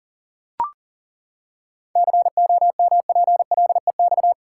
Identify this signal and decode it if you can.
Morse code, heard as a keyed tone, about 40 wpm, 700 Hz. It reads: COMPLEX
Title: Seven Letter Words - 40wpm